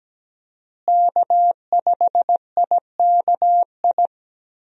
Morse code, heard as a keyed tone, about 17 wpm, 700 Hz. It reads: K5IKI
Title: USA Call Signs - 5 Characters - Repeated 3x - 17wpm